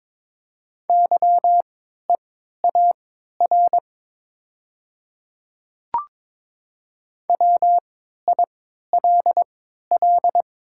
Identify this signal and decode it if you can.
Morse code, heard as a keyed tone, about 22 wpm, spaced as for 13 wpm, 700 Hz. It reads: YEAR WILL
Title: 3x Character Spacing - Top 200 Words - 22wpm